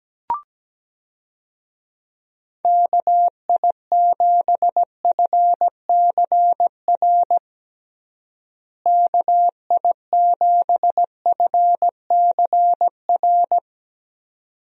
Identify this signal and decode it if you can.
Morse code, heard as a keyed tone, about 17 wpm, 700 Hz. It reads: KI7FCR KI7FCR
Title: USA Call Signs - 6 Characters - Repeated 2x - 17wpm